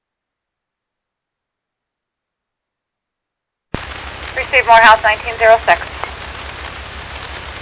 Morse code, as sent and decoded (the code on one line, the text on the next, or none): none